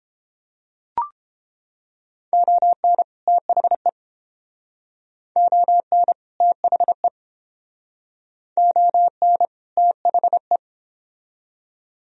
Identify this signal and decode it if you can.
Morse code, heard as a keyed tone, about 30 wpm, 700 Hz. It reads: ON THE ON THE ON THE